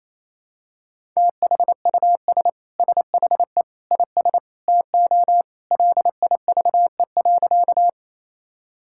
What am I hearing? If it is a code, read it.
Morse code, heard as a keyed tone, about 27 wpm, 700 Hz. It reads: THUS SHE IS TO LIVE.
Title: Sentences from Top 1000 Words - Review Part 5 - 28wpm